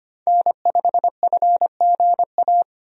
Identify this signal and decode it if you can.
Morse code, heard as a keyed tone, about 25 wpm, 700 Hz. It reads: N5FGA
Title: USA Call Signs - 5 Characters - 25wpm